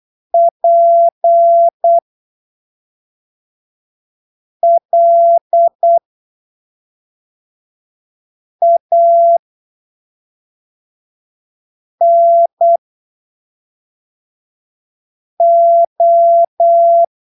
Morse code, heard as a keyed tone, about 8 wpm, spaced as for 3 wpm, 680 Hz. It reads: PLANO
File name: arquivos / texto / texto 07 08ppm